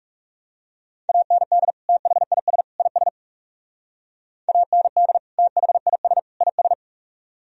Morse code, heard as a keyed tone, about 36 wpm, 700 Hz. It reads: AND THIS IS AND THIS IS
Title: Speed-Racing - Sets of 3 Words 30wpm